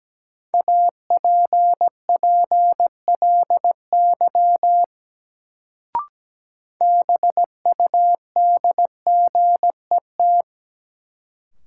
Morse code, heard as a keyed tone, about 17 wpm, 700 Hz. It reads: APPLY BUDGET
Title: Sentences from Top 1200 Words - New Word Drill - 17wpm